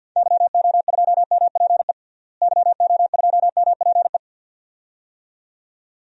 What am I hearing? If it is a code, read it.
Morse code, heard as a keyed tone, about 53 wpm, 700 Hz. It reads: YO2MPE YO2MPE
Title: DXCC-100 Call Signs - 6 Characters - Repeated 2x - 50wpm